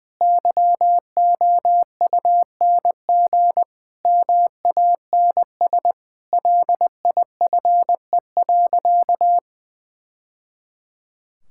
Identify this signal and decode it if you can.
Morse code, heard as a keyed tone, about 20 wpm, 700 Hz. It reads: YOUNG MANS LIFE.
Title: Sentences from Top 1000 Words - Review Part 6 - 20wpm